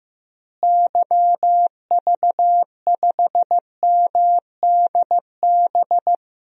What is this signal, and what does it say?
Morse code, heard as a keyed tone, about 15 wpm, 700 Hz. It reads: YV5MDB
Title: DXCC-100 Call Signs - 6 Characters - Repeated 2x - 15wpm